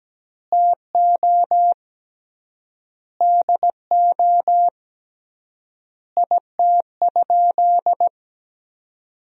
Morse code, heard as a keyed tone, about 17 wpm, 700 Hz. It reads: TO DO IT?
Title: 3x Word-Spacing - Sentences from Top 200 Words - 17wpm